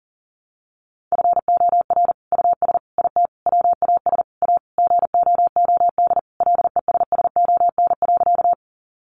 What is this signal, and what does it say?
Morse code, heard as a keyed tone, about 40 wpm, 700 Hz. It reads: FOR US IT WAS A GOOD LESSON.
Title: Sentences from Top 1200 Words 40wpm